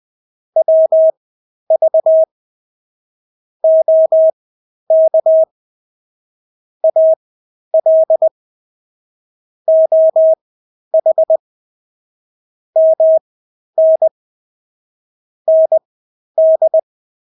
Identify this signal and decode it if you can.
Morse code, heard as a keyed tone, about 20 wpm, spaced as for 11 wpm, 640 Hz. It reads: WV OK AL OH MN ND
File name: State-Abbreviations-10